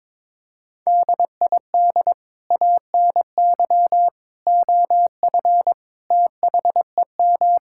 Morse code, heard as a keyed tone, about 22 wpm, 700 Hz. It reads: DID ANY OF THEM
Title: Sentences from Top 1600 Words 22wpm